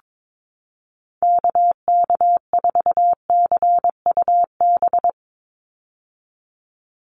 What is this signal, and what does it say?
Morse code, heard as a keyed tone, about 22 wpm, 700 Hz. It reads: KK4CUB
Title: USA Call Signs 22wpm